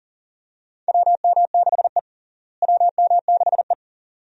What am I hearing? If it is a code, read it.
Morse code, heard as a keyed tone, about 40 wpm, 700 Hz. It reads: WM6E WM6E
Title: USA Call Signs - 4 Characters - Repeated 2x - 40wpm